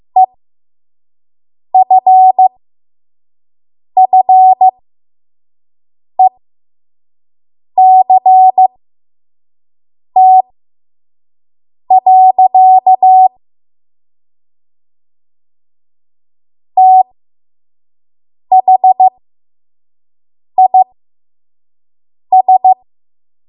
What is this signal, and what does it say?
Morse code, heard as a keyed tone, about 15 wpm, spaced as for 5 wpm, 750 Hz. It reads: EFFECT. THIS
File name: Morse / 05WPM